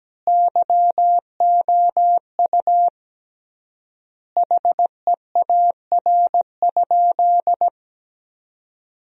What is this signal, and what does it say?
Morse code, heard as a keyed tone, about 17 wpm, 700 Hz. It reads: YOU HEAR?
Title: 3x Word-Spacing - Sentences from Top 300 Words - 17wpm